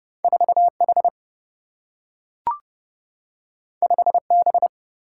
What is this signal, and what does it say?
Morse code, heard as a keyed tone, about 30 wpm, 700 Hz. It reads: 4H 5B
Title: Single-Letter Single-Number 30wpm